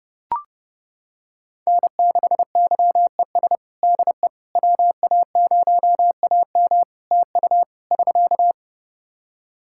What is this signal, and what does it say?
Morse code, heard as a keyed tone, about 30 wpm, 700 Hz. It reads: N6YES DE WA0AM TU <SK>
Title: Practice QSO - Exchange 4 - Ending - 30wpm